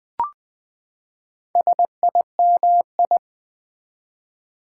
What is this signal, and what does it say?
Morse code, heard as a keyed tone, about 20 wpm, 700 Hz. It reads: SIMI
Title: Build A Word - 4 Letters At A Time - 10-Letter Words - 20wpm